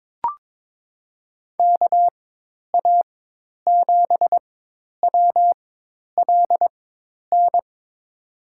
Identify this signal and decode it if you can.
Morse code, heard as a keyed tone, about 22 wpm, spaced as for 10 wpm, 700 Hz. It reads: KA7WLN
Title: USA Call Signs - 6 Characters - 4x Character Spacing - 22wpm